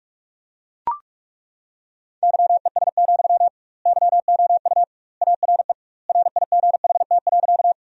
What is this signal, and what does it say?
Morse code, heard as a keyed tone, about 41 wpm, 700 Hz. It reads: YES, YOU ARE RIGHT.
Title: 2x Word-Spacing - Sentences from Top 500 Words - 45wpm